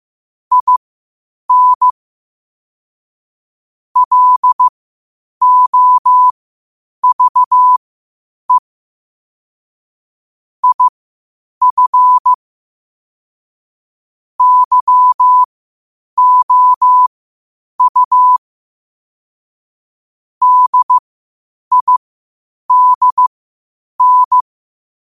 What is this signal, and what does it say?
Morse code, heard as a keyed tone, about 15 wpm, spaced as for 8 wpm, 1000 Hz. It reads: IN LOVE IF YOU DIDN
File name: qotdfiles / 8wpm